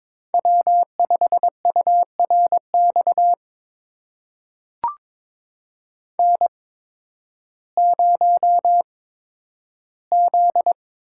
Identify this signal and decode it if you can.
Morse code, heard as a keyed tone, about 22 wpm, spaced as for 12 wpm, 700 Hz. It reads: W5URX N0Z
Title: USA Call Signs - 5 Characters - 8x Character Spacing - 22wpm